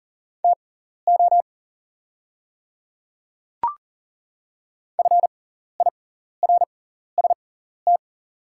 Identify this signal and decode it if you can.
Morse code, heard as a keyed tone, about 39 wpm, spaced as for 14 wpm, 700 Hz. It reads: TO FIRST
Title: 6x Character Spacing - Top 100 Words - 40wpm